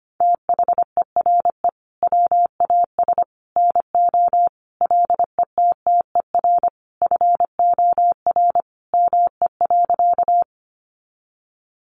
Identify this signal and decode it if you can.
Morse code, heard as a keyed tone, about 25 wpm, 700 Hz. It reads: THERE WAS NO LETTER FOR ME.
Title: Sentences from Top 300 Words 25wpm